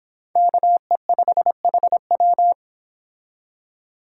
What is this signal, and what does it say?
Morse code, heard as a keyed tone, about 25 wpm, 700 Hz. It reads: KE5HW